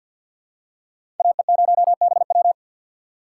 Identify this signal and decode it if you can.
Morse code, heard as a keyed tone, about 48 wpm, 700 Hz. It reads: AE0BW
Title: USA Call Signs - 5 Characters - 4x Character Spacing - 50wpm